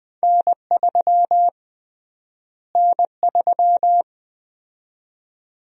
Morse code, heard as a keyed tone, about 21 wpm, 700 Hz. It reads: N3 N3
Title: USA Call Sign Prefixes - Repeated 2x - 20wpm